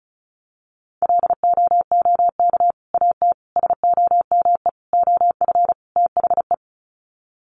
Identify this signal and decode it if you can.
Morse code, heard as a keyed tone, about 34 wpm, 700 Hz. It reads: LOOK AT SOME OF THE
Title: Sets of 5 Words Encore 35wpm